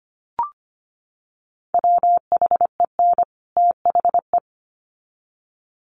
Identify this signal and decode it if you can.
Morse code, heard as a keyed tone, about 25 wpm, 700 Hz. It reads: WHEN THE
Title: Sets of 2 Words 25wpm